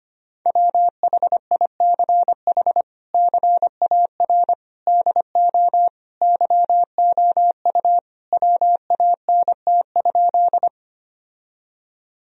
Morse code, heard as a keyed tone, about 25 wpm, 700 Hz. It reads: WHICH CAR DO YOU WANT?
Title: Sentences from Top 1000 Words - Review Part 1 - 25wpm